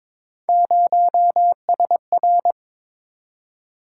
Morse code, heard as a keyed tone, about 22 wpm, 700 Hz. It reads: 0SR
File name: practice / 22wpm / Three Letter-Number Combinations - 22wpm